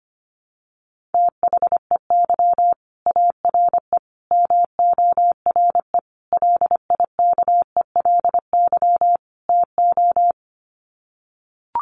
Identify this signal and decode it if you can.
Morse code, heard as a keyed tone, about 25 wpm, 700 Hz. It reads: THEY ARE MORE LIKELY TO
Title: Sets of 5 Words Encore2 25wpm